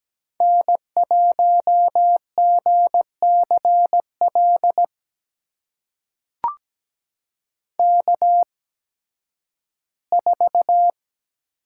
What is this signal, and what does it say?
Morse code, heard as a keyed tone, about 17 wpm, spaced as for 11 wpm, 700 Hz. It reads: N1GCL K4
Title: USA Call Signs - 5 Characters - 8x Character Spacing - 17wpm